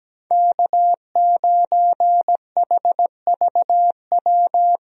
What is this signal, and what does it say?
Morse code, heard as a keyed tone, about 17 wpm, 700 Hz. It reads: K9HVW
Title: USA Call Signs - 5 Characters - Repeated 2x - 17wpm